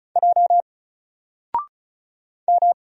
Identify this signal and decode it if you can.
Morse code, heard as a keyed tone, about 34 wpm, 700 Hz. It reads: J M
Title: Single Letter-Number 35wpm